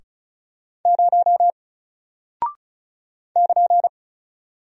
Morse code, heard as a keyed tone, about 36 wpm, 700 Hz. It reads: 0 <KN>
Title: Single Character (Letter:Number:Punctuation:Prosign) Alt 35wpm